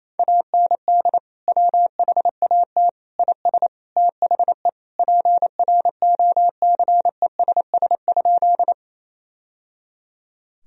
Morse code, heard as a keyed tone, about 28 wpm, 700 Hz. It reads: AND WHAT IS THE PROCESS?
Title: Sentences from Top 1000 Words - Review Part 3 - 28wpm